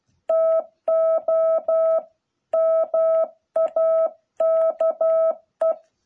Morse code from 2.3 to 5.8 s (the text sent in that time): MAKE